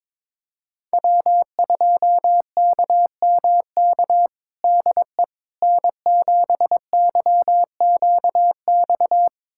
Morse code, heard as a keyed tone, about 22 wpm, 700 Hz. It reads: W2KMK DE N7YQX
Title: USA Callsign DE Callsign - 5 Characters - 22wpm